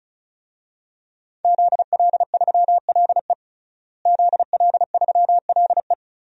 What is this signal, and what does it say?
Morse code, heard as a keyed tone, about 34 wpm, 700 Hz. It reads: ZL3LE ZL3LE
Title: DXCC-100 Call Signs - 5 Characters - Repeated 2x - 35wpm